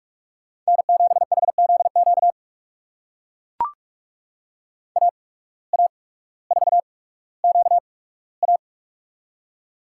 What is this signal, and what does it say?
Morse code, heard as a keyed tone, about 42 wpm, spaced as for 19 wpm, 700 Hz. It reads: N7HZQ AA4QA
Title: USA Call Signs - 5 Characters - 8x Character Spacing - 45wpm